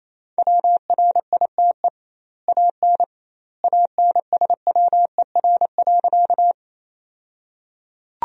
Morse code, heard as a keyed tone, about 28 wpm, 700 Hz. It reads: WRITE AN ANSWER.
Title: 2x Word-Spacing - Sentences from Top 300 Words - 28wpm